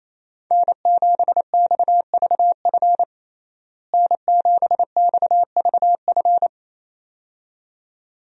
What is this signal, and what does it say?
Morse code, heard as a keyed tone, about 28 wpm, 700 Hz. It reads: N7XVF N7XVF
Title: USA Call Signs - 5 Characters - Repeated 2x - 28wpm